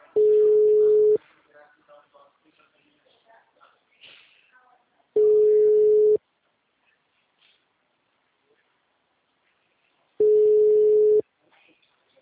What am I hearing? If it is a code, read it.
Ringback tone (Europe, 425 Hz), with speech.